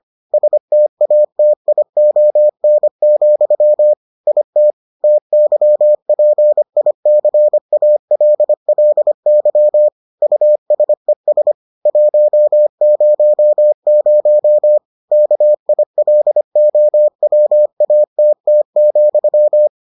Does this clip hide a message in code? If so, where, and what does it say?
Yes: Morse code from 0.3 to 19.7 s, heard as a keyed tone, about 25 wpm, 600 Hz. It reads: STATION, IT TYPICALLY USES 100 KILOWATT,